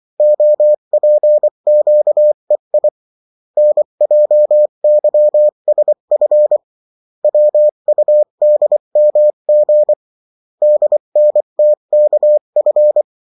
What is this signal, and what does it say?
Morse code, heard as a keyed tone, about 24 wpm, 600 Hz. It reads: OPQEI NJYSF WUDMG DNTKF
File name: morse mp3 / revision9-vitesse 22 mots